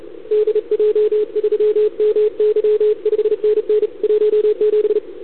Morse code, heard as a keyed tone, about 34 wpm, 440 Hz, with radio noise. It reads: DJ3MY 5NN 17
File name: CQWWCW2021 / UN9L